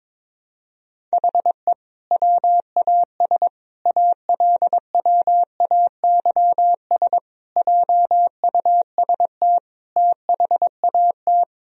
Morse code, heard as a keyed tone, about 22 wpm, 700 Hz. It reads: HE WAS ALWAYS JUST THAT
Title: Sentences from Top 1000 Words - Review Part 4 - 22wpm